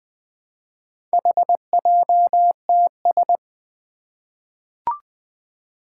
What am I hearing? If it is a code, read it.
Morse code, heard as a keyed tone, about 20 wpm, 700 Hz. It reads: HJTS
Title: Four Challenging Letter-Number Combinations 20wpm